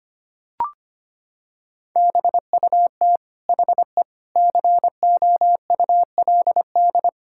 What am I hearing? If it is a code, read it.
Morse code, heard as a keyed tone, about 25 wpm, 700 Hz. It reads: BUT HE COULD
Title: Sentences from Top 1600 Words 25wpm